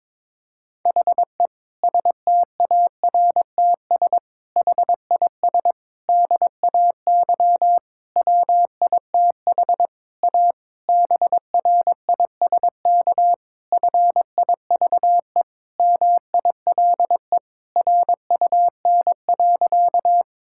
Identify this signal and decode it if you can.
Morse code, heard as a keyed tone, about 22 wpm, 700 Hz. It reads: HE STARTS HIS DAY WITH A BRISK FIVE MILE RUN.
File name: Difficult Sentences with Hints - 22wpm